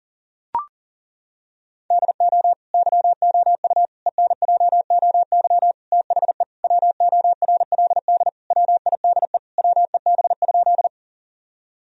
Morse code, heard as a keyed tone, about 37 wpm, 700 Hz. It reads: DO YOU ENJOY THE WORLD WIDE WEB?
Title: Sentences from Top 1000 Words - Review Part 4 - 40wpm